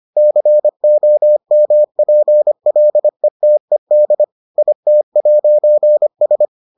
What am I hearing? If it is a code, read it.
Morse code, heard as a keyed tone, about 25 wpm, 600 Hz. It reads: COMPLETED IT'S